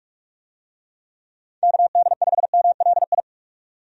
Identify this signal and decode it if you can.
Morse code, heard as a keyed tone, about 47 wpm, 700 Hz. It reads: KD5MLI
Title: USA Call Signs - 6 Characters - 45wpm